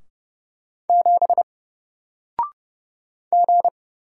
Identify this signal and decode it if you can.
Morse code, heard as a keyed tone, about 29 wpm, 700 Hz. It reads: 7 G
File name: Single-Letter-Number-30wpm